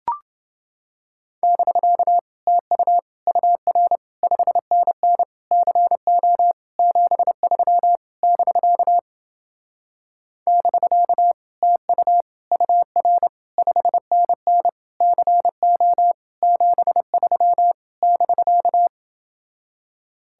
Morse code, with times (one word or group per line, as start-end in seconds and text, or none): 1.4-2.2 <BK>
2.5-3.0 TU
3.3-4.0 UR
4.2-5.2 5NN
5.5-6.5 CO
6.8-8.0 73
8.2-9.0 <BK>
10.5-11.3 <BK>
11.6-12.2 TU
12.5-13.3 UR
13.6-14.7 5NN
15.0-16.1 CO
16.4-17.7 73
18.0-18.9 <BK>